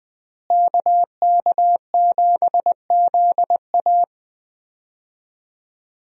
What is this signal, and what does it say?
Morse code, heard as a keyed tone, about 20 wpm, 700 Hz. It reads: KK7ZA
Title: USA Call Signs - 5 Characters - Repeated 3x - 20wpm